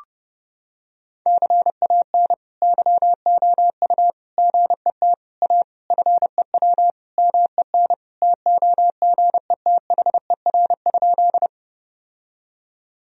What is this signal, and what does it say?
Morse code, heard as a keyed tone, about 30 wpm, 700 Hz. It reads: CAN YOU GET A FEW MEN TOGETHER?
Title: Sentences from Top 1000 Words - Review Part 1 - 30wpm